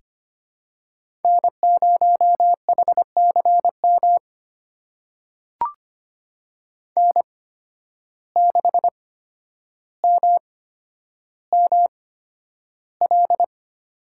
Morse code, heard as a keyed tone, about 25 wpm, spaced as for 11 wpm, 700 Hz. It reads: N0HCM N6MML
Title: USA Call Signs - 5 Characters - 8x Character Spacing - 25wpm